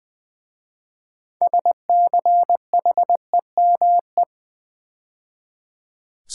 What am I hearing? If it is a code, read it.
Morse code, heard as a keyed tone, about 21 wpm, 700 Hz. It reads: SCHEME
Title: 6 Letter Words - First 3 Letters Given - 20wpm